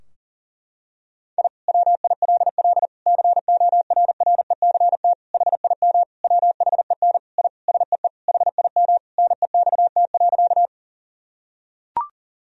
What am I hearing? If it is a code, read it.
Morse code, heard as a keyed tone, about 40 wpm, 700 Hz. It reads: I WILL CORRECT HIM WHEN I SEE HIM NEXT.
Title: Sentences from Top 1000 Words - Review Part 2 - 40wpm